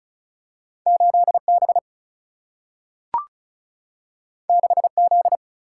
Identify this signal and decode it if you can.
Morse code, heard as a keyed tone, about 36 wpm, 700 Hz. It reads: 8B 6Z
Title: Single-Letter Single-Number 35wpm